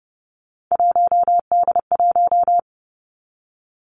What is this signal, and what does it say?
Morse code, heard as a keyed tone, about 30 wpm, 700 Hz. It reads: 1D1
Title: Three Challenging Letter-Number Combinations 30wpm